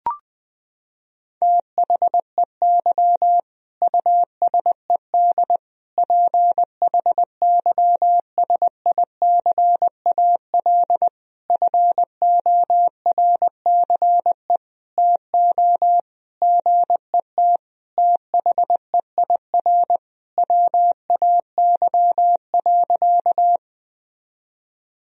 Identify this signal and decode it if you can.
Morse code, heard as a keyed tone, about 20 wpm, 700 Hz. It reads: THEY USED PHYSICAL FORCE TO GET THEIR WAY.
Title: Sentences from Top 1000 Words - Review Part 4 - 20wpm